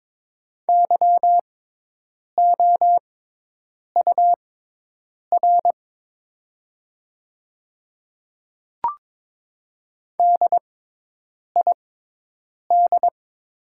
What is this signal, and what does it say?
Morse code, heard as a keyed tone, about 22 wpm, spaced as for 8 wpm, 700 Hz. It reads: YOUR DID
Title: 6x Character Spacing - Top 100 Words - 22wpm